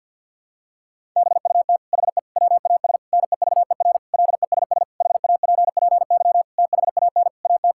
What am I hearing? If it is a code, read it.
Morse code, heard as a keyed tone, about 52 wpm, 690 Hz. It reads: BUT HE WAS NEVER LESS HAPPY THAN AT
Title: Sentences from Top 1000 Words - Review Part 5 - 50wpm